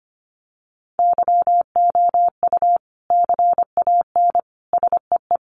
Morse code, heard as a keyed tone, about 25 wpm, 700 Hz. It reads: YOU CAN SEE
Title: Sets of 3 Words 25wpm